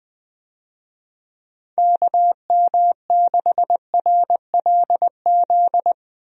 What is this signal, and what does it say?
Morse code, heard as a keyed tone, about 20 wpm, 700 Hz. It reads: KM6RLZ
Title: USA Call Signs - 6 Characters - Repeated 3x - 20wpm